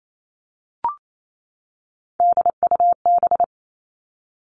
Morse code, heard as a keyed tone, about 28 wpm, 700 Hz. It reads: DUB
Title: Three Challenging Letter-Number Combinations 28wpm